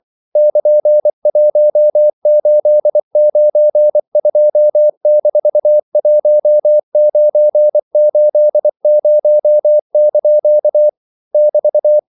Morse code, heard as a keyed tone, about 24 wpm, 600 Hz. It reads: <KN>1892-1980) <BT>
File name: CW / Practice / EnglishQuotations0324